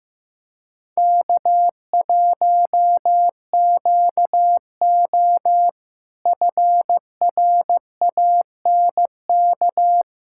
Morse code, heard as a keyed tone, about 15 wpm, 700 Hz. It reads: K1QO FRANK
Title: K1USN SST QSO Practice - 15wpm